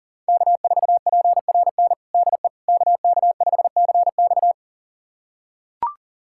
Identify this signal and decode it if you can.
Morse code, heard as a keyed tone, about 40 wpm, 700 Hz. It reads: K4PRN DE KK5CX
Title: USA Callsign DE Callsign - 5 Characters - 40wpm